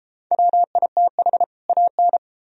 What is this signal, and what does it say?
Morse code, heard as a keyed tone, about 32 wpm, 700 Hz. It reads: WITH AN